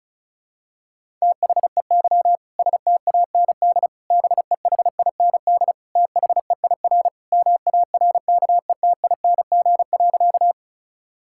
Morse code, heard as a keyed tone, about 36 wpm, 700 Hz. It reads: THEY STAND BEHIND THEIR MARKETING.
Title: Sentences from Top 1000 Words - Review Part 5 - 35wpm